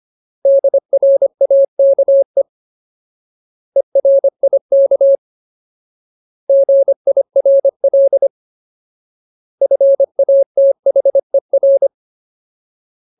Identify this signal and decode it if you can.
Morse code, heard as a keyed tone, about 25 wpm, 550 Hz. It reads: DRAKE ERIK GIRL FATHER
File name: Ultimate Word List / 25WPM Ultimate